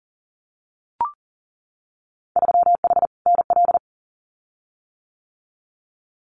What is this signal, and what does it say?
Morse code, heard as a keyed tone, about 40 wpm, 700 Hz. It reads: 3H NL
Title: Winter Field Day Exchanges Encore 40wpm